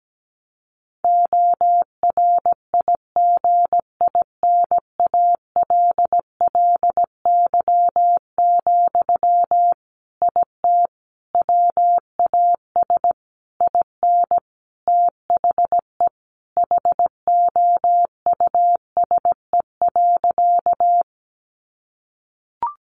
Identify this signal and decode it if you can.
Morse code, heard as a keyed tone, about 17 wpm, 700 Hz. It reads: ORIGINALLY, IT WAS IN THE HOUSE.
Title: Sentences from Top 1500 Words 17wpm